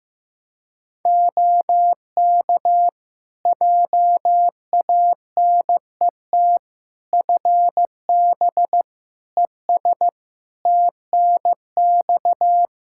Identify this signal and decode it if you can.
Morse code, heard as a keyed tone, about 15 wpm, 700 Hz. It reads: OK JANET FB ES TNX
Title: Practice QSO - Exchange 2 - Info - 15wpm